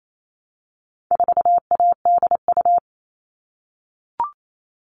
Morse code, heard as a keyed tone, about 28 wpm, 700 Hz. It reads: 4ADU
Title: All Cut Numbers - 4-Digit 28wpm